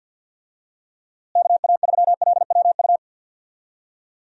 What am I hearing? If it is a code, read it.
Morse code, heard as a keyed tone, about 48 wpm, 700 Hz. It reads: KA3LWU
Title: USA Call Signs - 6 Characters - Repeated 2x - 50wpm